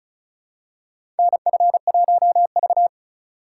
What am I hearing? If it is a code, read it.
Morse code, heard as a keyed tone, about 36 wpm, 700 Hz. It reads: NF1V
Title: USA Call Signs - 2x Character Spacing - 35wpm